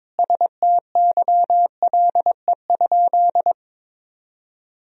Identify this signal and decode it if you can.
Morse code, heard as a keyed tone, about 22 wpm, 700 Hz. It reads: STYLE?
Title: Sentences from Top 1000 Words - Review Part 4 - 22wpm